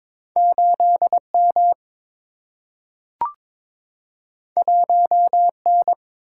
Morse code, heard as a keyed tone, about 22 wpm, 700 Hz. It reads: 8 M 1 N
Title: Single-Letter Single-Number 22wpm